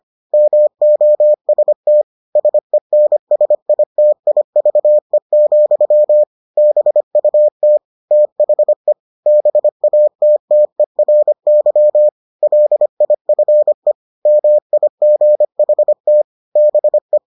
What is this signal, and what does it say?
Morse code, heard as a keyed tone, about 25 wpm, 600 Hz. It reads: MOST SENSITIVE, BUT THE BATTERY LIFE MIGHT BE